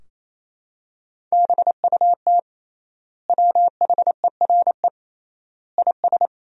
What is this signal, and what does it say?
Morse code, heard as a keyed tone, about 28 wpm, 700 Hz. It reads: BUT WHERE IS
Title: 3x Word-Spacing - Sentences from Top 200 Words - 28wpm